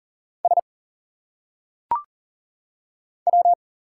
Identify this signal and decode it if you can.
Morse code, heard as a keyed tone, about 40 wpm, 700 Hz. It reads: S W
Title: Single Letters - Phonetic Pronunciation 40wpm